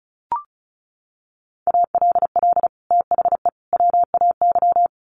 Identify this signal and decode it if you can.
Morse code, heard as a keyed tone, about 34 wpm, 700 Hz. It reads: ALL THE WAY
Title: Sets of 3 Words 35wpm